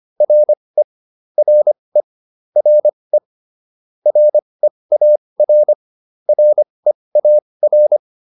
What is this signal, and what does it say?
Morse code, heard as a keyed tone, about 25 wpm, spaced as for 20 wpm, 600 Hz. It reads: RE RE RE REAR REAR